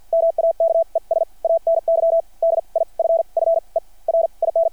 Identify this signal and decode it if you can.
Morse code, heard as a keyed tone, about 53 wpm, spaced as for 38 wpm, 660 Hz. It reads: MAKES ANY DIFFERIN
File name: archived / CRSnet